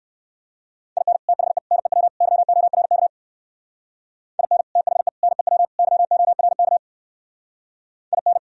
Morse code, heard as a keyed tone, about 56 wpm, 700 Hz. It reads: IN THE NEW YORK IN THE NEW YORK IN